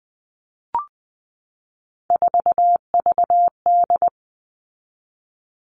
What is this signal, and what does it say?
Morse code, heard as a keyed tone, about 20 wpm, 700 Hz. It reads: 4VD
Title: All Cut Numbers - 3-Digit 20wpm